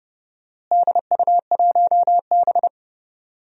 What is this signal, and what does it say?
Morse code, heard as a keyed tone, about 31 wpm, 700 Hz. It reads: DU1B
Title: DXCC-100 Call Signs - 4 Characters - Repeated 3x - 30wpm